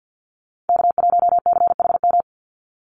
Morse code, heard as a keyed tone, about 50 wpm, 700 Hz. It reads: K1CHM
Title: Call Signs Repeated 2x - 50wpm